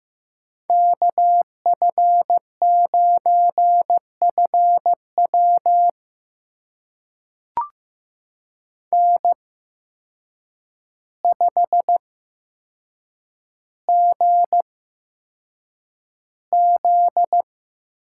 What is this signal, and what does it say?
Morse code, heard as a keyed tone, about 15 wpm, spaced as for 7 wpm, 700 Hz. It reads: KF9FW N5GZ
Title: USA Call Signs - 5 Characters - 8x Character Spacing - 15wpm